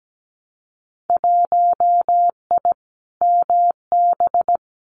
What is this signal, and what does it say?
Morse code, heard as a keyed tone, about 17 wpm, 700 Hz. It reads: 1I MB
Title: Winter Field Day Exchanges Encore 17wpm